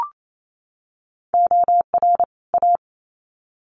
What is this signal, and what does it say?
Morse code, heard as a keyed tone, about 28 wpm, 700 Hz. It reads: OR A
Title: Sets of 2 Words 28wpm